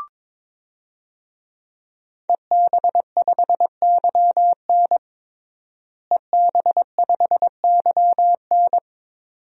Morse code, heard as a keyed tone, about 22 wpm, 700 Hz. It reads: EB5YN EB5YN
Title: DXCC-100 Call Signs - 5 Characters - Repeated 2x - 22wpm